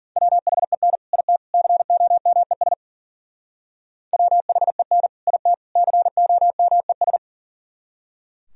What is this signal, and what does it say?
Morse code, heard as a keyed tone, about 51 wpm, 700 Hz. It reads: WHEN IT COMES WHEN IT COMES